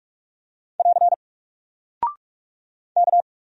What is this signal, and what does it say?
Morse code, heard as a keyed tone, about 44 wpm, 700 Hz. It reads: <AR> K
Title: Single Character (Letter:Number:Punctuation:Prosign) 45wpm